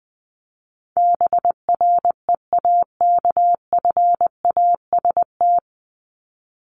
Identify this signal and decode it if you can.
Morse code, heard as a keyed tone, about 20 wpm, 700 Hz. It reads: BREAKFAST
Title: Top 1251-1500 Words 20wpm